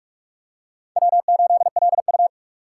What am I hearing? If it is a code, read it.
Morse code, heard as a keyed tone, about 46 wpm, 700 Hz. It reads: W8LU
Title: USA Call Signs - 6x Character Spacing - 45wpm